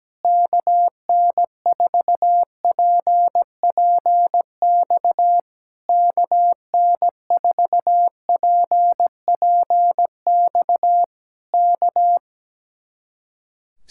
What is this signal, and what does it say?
Morse code, heard as a keyed tone, about 17 wpm, 700 Hz. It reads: KN4PPX KN4PPX K
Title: Speed-Racing Common QSO Phrases Encore - 17wpm